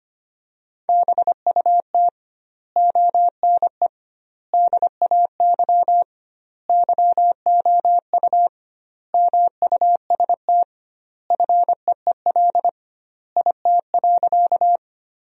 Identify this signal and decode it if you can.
Morse code, heard as a keyed tone, about 25 wpm, 700 Hz. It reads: BUT ONE DAY YOU MUST FEEL IT.
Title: 2x Word-Spacing - Sentences from Top 400 Words - 25wpm